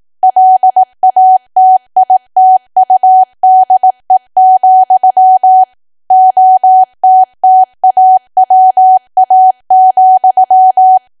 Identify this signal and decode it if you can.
Morse code, heard as a keyed tone, about 18 wpm, 750 Hz. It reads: LATITUDE, OTTAWA,
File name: cwa / W1AW-Level2 / 18WPM